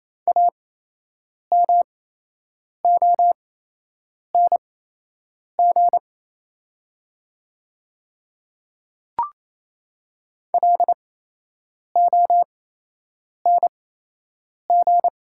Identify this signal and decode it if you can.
Morse code, heard as a keyed tone, about 28 wpm, spaced as for 8 wpm, 700 Hz. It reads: AMONG LONG
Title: 8x Character Spacing - Top 500 Words - 28wpm